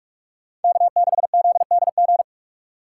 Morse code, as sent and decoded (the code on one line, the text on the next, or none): -.- -.... --.. -.. --.
K6ZDG